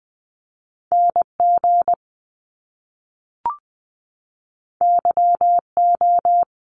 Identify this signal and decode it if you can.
Morse code, heard as a keyed tone, about 20 wpm, 700 Hz. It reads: NG YO
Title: Top Two Letter Combinations in Words 20wpm